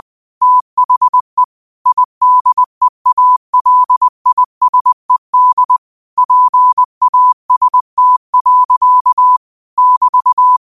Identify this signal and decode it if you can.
Morse code, heard as a keyed tone, about 20 wpm, 1000 Hz. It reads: THE IDEALISED PAST. <BT>
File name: qotdfiles / 20wpm